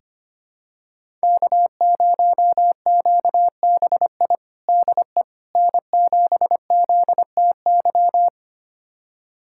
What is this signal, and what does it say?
Morse code, heard as a keyed tone, about 25 wpm, 700 Hz. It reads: K0QBI DE N7ZTY
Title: USA Callsign DE Callsign - 5 Characters - 25wpm